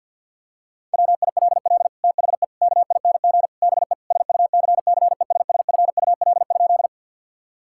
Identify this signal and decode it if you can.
Morse code, heard as a keyed tone, about 51 wpm, 700 Hz. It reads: WILL THE KING BE SUCCESSFUL?
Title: Sentences from Top 1000 Words - Review Part 6 - 50wpm